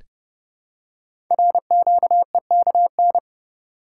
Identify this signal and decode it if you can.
Morse code, heard as a keyed tone, about 30 wpm, 700 Hz. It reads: RQEKN
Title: Code Groups - 5 Letters - 30wpm